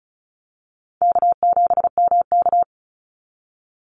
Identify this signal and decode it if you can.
Morse code, heard as a keyed tone, about 34 wpm, 700 Hz. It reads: K7MK
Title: NA SOTA Activator Callsigns 35wpm